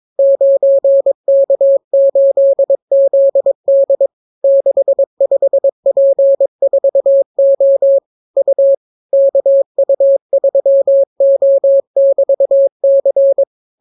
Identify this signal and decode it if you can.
Morse code, heard as a keyed tone, about 22 wpm, 550 Hz. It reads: 9K8ZD 65P4O U KU3O<BT>C